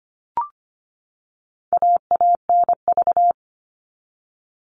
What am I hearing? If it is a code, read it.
Morse code, heard as a keyed tone, about 25 wpm, 700 Hz. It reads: AANV
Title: All Cut Numbers - 4-Digit 25wpm